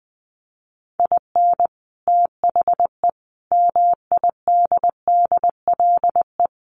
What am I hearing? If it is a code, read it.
Morse code, heard as a keyed tone, about 20 wpm, 700 Hz. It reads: IN THE MIDDLE
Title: Sets of 5 Words 20wpm